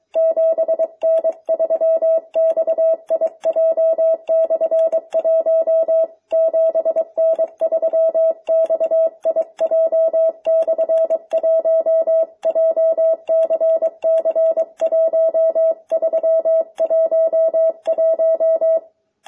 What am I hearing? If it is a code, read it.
Morse code, heard as a keyed tone, about 23 wpm, 660 Hz. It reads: 7N3XIJ/1 7N3XIJ/1 JCC1311